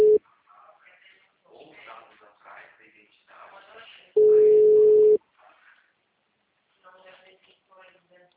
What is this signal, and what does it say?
Ringback tone (Europe, 425 Hz), with speech.